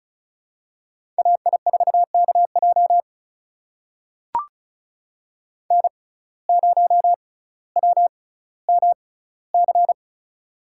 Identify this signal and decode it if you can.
Morse code, heard as a keyed tone, about 36 wpm, spaced as for 18 wpm, 700 Hz. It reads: AI4KJ N0WMC
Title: USA Call Signs - 5 Characters - 6x Character Spacing - 35wpm